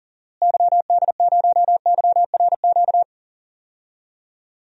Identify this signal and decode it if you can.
Morse code, heard as a keyed tone, about 38 wpm, 700 Hz. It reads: YD0YRQ
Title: DXCC-100 Call Signs - 6 Characters - Repeated 3x - 40wpm